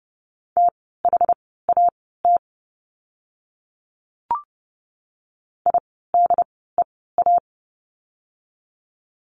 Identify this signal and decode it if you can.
Morse code, heard as a keyed tone, about 30 wpm, spaced as for 17 wpm, 700 Hz. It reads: THAT IDEA
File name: course / 30:10wpm / 108 Lesson 18f - Letter m - Words 30wpm